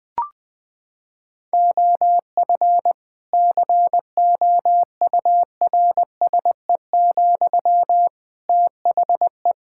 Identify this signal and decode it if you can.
Morse code, heard as a keyed tone, about 21 wpm, 700 Hz. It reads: OF COURSE, THE
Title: Sentences from Top 1600 Words 20wpm